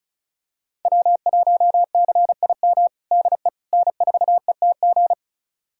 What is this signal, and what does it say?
Morse code, heard as a keyed tone, about 34 wpm, 700 Hz. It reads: W1CIM DE N4ETG
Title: USA Callsign DE Callsign - 5 Characters - 35wpm